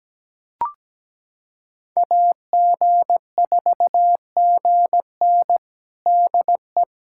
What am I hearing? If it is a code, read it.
Morse code, heard as a keyed tone, about 17 wpm, 700 Hz. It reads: AG4GN DE
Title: USA Callsign DE Callsign - 5 Characters - 17wpm